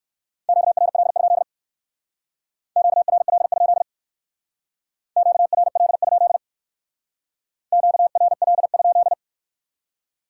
Morse code, heard as a keyed tone, about 58 wpm, 700 Hz. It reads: QRL? QRL? QRL? QRL?